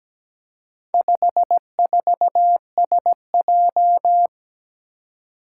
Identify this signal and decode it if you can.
Morse code, heard as a keyed tone, about 17 wpm, 700 Hz. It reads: 54SJ
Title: Four Challenging Letter-Number Combinations 17wpm